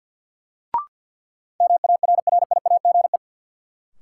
Morse code, heard as a keyed tone, about 52 wpm, 700 Hz. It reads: MARRIAGE
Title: Sentences from Top 800 Words - New Word Drill - 50wpm